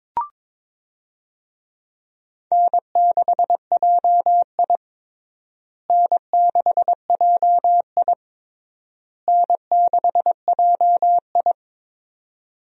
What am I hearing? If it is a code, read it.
Morse code, heard as a keyed tone, about 22 wpm, 700 Hz. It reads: N6JI N6JI N6JI
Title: USA Call Signs - 4 Characters - Repeated 3x - 22wpm